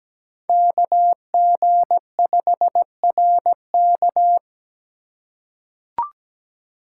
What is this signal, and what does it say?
Morse code, heard as a keyed tone, about 17 wpm, 700 Hz. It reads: KG5RK
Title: USA Call Signs - 5 Characters - 3x Character Spacing - 17wpm